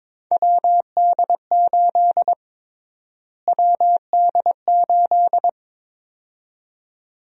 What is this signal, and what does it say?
Morse code, heard as a keyed tone, about 22 wpm, 700 Hz. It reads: WD8 WD8
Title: USA Call Sign Prefixes - Repeated 2x - 22wpm